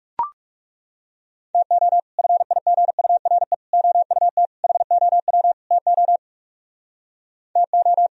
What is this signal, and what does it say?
Morse code, heard as a keyed tone, about 39 wpm, 700 Hz. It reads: TO FIGURE OUT HOW TO TO